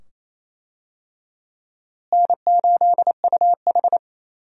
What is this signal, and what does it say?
Morse code, heard as a keyed tone, about 28 wpm, 700 Hz. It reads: N8UH
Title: USA Call Signs - 4 Characters - Repeated 2x - 28wpm